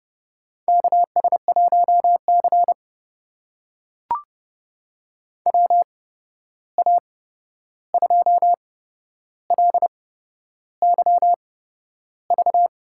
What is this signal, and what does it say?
Morse code, heard as a keyed tone, about 30 wpm, spaced as for 11 wpm, 700 Hz. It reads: KS1C WA2LYV
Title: USA Call Signs - 8x Character Spacing - 30wpm